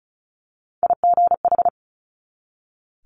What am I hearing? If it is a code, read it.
Morse code, heard as a keyed tone, about 36 wpm, 700 Hz. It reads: IGH
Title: Multi-letter Phonemes 35wpm